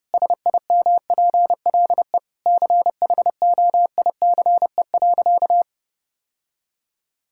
Morse code, heard as a keyed tone, about 32 wpm, 700 Hz. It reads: SIMPLE CHOICE.
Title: Sentences from Top 1000 Words - Review Part 6 - 30wpm